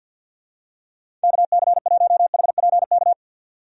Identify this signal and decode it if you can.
Morse code, heard as a keyed tone, about 46 wpm, 700 Hz. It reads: KC1HPK
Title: USA Call Signs - 6 Characters - 50wpm